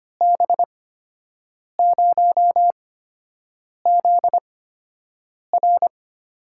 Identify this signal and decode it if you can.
Morse code, heard as a keyed tone, about 25 wpm, spaced as for 7 wpm, 700 Hz. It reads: B0ZR
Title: USA Call Signs - 6 Characters - 8x Character Spacing - 25wpm